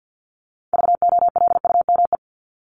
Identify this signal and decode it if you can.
Morse code, heard as a keyed tone, about 50 wpm, 700 Hz. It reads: VOLUME
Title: Top 1751-2000 Words 50wpm